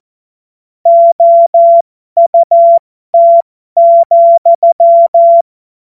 Morse code, heard as a keyed tone, about 14 wpm, spaced as for 12 wpm, 680 Hz. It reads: OUT,